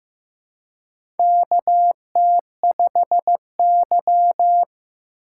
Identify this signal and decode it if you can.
Morse code, heard as a keyed tone, about 15 wpm, 700 Hz. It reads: KT5Y
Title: USA Call Signs - 4 Characters - 3x Character Spacing - 15wpm